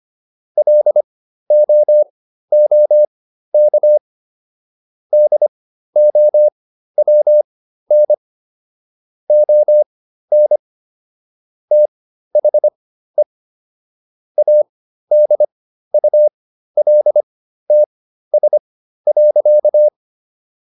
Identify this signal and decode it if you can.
Morse code, heard as a keyed tone, about 25 wpm, spaced as for 13 wpm, 600 Hz. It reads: LOOK DOWN ON THE ADULTS.